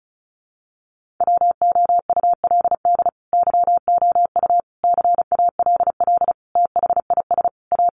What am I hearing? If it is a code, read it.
Morse code, heard as a keyed tone, about 34 wpm, 700 Hz. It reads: WOULD YOU CALL THIS A
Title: Sentences from Top 600 Words 35wpm